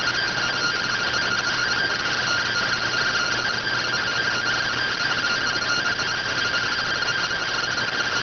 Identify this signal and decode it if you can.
Two-tone teleprinter signal, about 170 Hz shift.